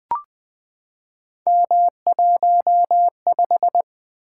Morse code, heard as a keyed tone, about 20 wpm, 700 Hz. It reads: M15
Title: Three Letter-Number Combinations - 20wpm